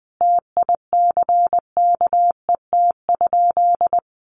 Morse code, heard as a keyed tone, about 19 wpm, 700 Hz. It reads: TICKET?
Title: Sentences from Top 1400 Words 20wpm